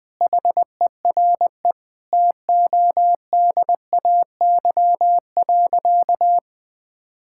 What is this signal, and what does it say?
Morse code, heard as a keyed tone, about 21 wpm, 700 Hz. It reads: HERE TODAY.
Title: Sentences with 7-Word Phrases - 20wpm